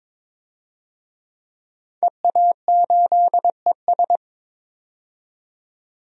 Morse code, heard as a keyed tone, about 21 wpm, 700 Hz. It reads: EA8ES
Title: DXCC-100 Call Signs - 5 Characters - 22wpm